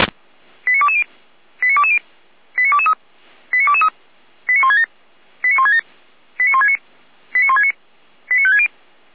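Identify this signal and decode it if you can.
Five-tone selective call (ZVEI), heard as quick runs of five tones, tones of about 70 ms. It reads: 89220 89220 89223 89223 89167 89167 89169 89169 89560
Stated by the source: ZVEI five-tone selective call received by SDR, from fire brigade Feuerwehr Orlamünde.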